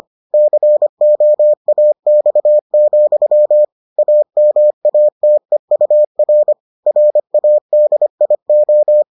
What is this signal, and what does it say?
Morse code, heard as a keyed tone, about 25 wpm, 600 Hz. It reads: COAX, AMATEUR RADIO